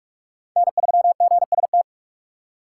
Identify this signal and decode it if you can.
Morse code, heard as a keyed tone, about 44 wpm, 700 Hz. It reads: N3GST
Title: USA Call Signs - 5 Characters - 6x Character Spacing - 45wpm